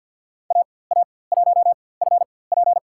Morse code, heard as a keyed tone, about 50 wpm, spaced as for 23 wpm, 700 Hz. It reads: AA1FP
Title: USA Call Signs - 4x Character Spacing - 50wpm